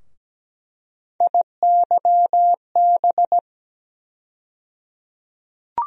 Morse code, heard as a keyed tone, about 17 wpm, 700 Hz. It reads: IYB
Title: USA Call Sign Postfixes - 3 Letters 17wpm